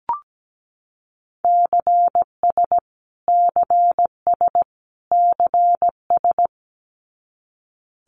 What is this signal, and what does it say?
Morse code, heard as a keyed tone, about 17 wpm, 700 Hz. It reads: CS CS CS
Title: Abbreviations Repeated 3x - 17wpm